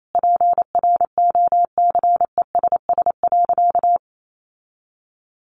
Morse code, heard as a keyed tone, about 28 wpm, 700 Hz. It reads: PROCESS.
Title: Sentences from Top 1400 Words 28wpm